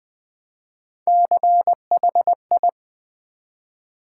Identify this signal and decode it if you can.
Morse code, heard as a keyed tone, about 20 wpm, 700 Hz. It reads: CHI
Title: Build A Word - 1 Letter At A Time - 4-Letter Words - 20wpm